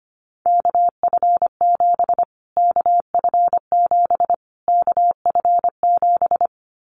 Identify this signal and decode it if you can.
Morse code, heard as a keyed tone, about 25 wpm, 700 Hz. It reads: KF7 KF7 KF7
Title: Callsign Prefixes for Top 25 DXCC - Repeated 3x - 25wpm